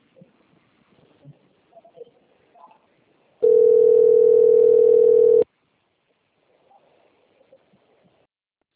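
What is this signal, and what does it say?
Ringback tone (North America), 440 + 480 Hz.
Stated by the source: Outbound phone call recorded on a system labelled Itatiba 3.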